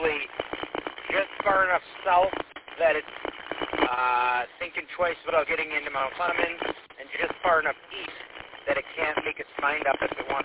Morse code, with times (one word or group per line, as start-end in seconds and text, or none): none